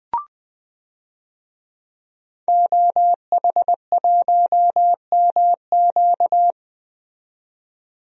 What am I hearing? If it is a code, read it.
Morse code, heard as a keyed tone, about 21 wpm, 700 Hz. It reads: OH1MQ
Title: DXCC-100 Call Signs - 5 Characters - 20wpm